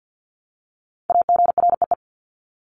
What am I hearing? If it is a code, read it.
Morse code, heard as a keyed tone, about 50 wpm, 700 Hz. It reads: AGREE
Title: Top 1001-1250 Words 50wpm